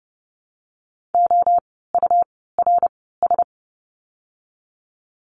Morse code, heard as a keyed tone, about 30 wpm, spaced as for 17 wpm, 700 Hz. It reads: OURS
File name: Lesson 17f - Letter u - Words 30wpm